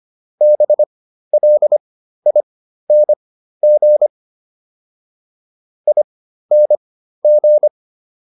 Morse code, heard as a keyed tone, about 25 wpm, spaced as for 13 wpm, 600 Hz. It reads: BLING ING